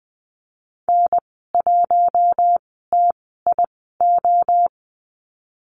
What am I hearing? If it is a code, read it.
Morse code, heard as a keyed tone, about 20 wpm, spaced as for 15 wpm, 700 Hz. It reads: N1TIO
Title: 042 Lesson 8g - Number 1 - Call Signs 20wpm